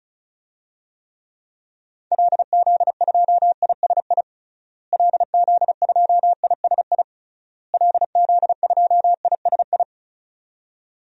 Morse code, heard as a keyed tone, about 36 wpm, 700 Hz. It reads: LZ2ISI LZ2ISI LZ2ISI
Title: DXCC-100 Call Signs - 6 Characters - Repeated 3x - 35wpm